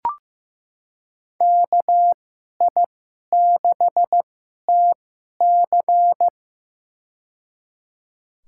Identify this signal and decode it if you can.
Morse code, heard as a keyed tone, about 16 wpm, spaced as for 11 wpm, 700 Hz. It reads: KI6TC
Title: USA Call Signs - 2x Character Spacing - 15wpm